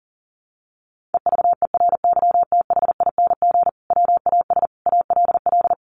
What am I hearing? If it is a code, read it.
Morse code, heard as a keyed tone, about 37 wpm, 700 Hz. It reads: EVERYTHING WAS ALL
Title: Sentences from Top 700 Words 40wpm